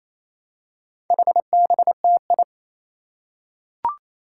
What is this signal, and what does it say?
Morse code, heard as a keyed tone, about 28 wpm, 700 Hz. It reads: HBTI
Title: Four Challenging Letter-Number Combinations 28wpm